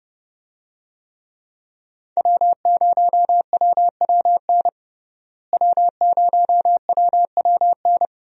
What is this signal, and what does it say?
Morse code, heard as a keyed tone, about 30 wpm, 700 Hz. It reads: W0WWN W0WWN
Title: USA Call Signs - 5 Characters - Repeated 3x - 30wpm